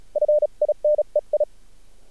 Morse code, heard as a keyed tone, about 34 wpm, 600 Hz. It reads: FINEI
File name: archived / CRSnet